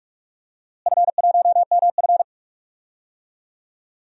Morse code, heard as a keyed tone, about 46 wpm, 700 Hz. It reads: F1MF
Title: DXCC-100 Call Signs - 4 Characters - Repeated 3x - 45wpm